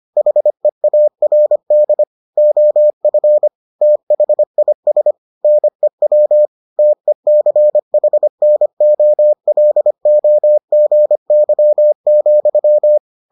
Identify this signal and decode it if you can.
Morse code, heard as a keyed tone, about 25 wpm, 600 Hz. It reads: HEARD OF THIS NEW TECHNOLOGY,